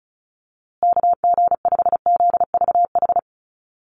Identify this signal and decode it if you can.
Morse code, heard as a keyed tone, about 34 wpm, 700 Hz. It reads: KG5ZVH
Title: USA Call Signs Repeated 2x 35wpm